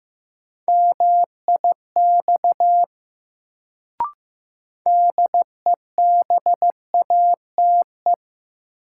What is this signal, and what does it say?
Morse code, heard as a keyed tone, about 15 wpm, 700 Hz. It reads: MIX DEBATE
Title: Sentences from Top 1100 Words - New Word Drill Alt - 15wpm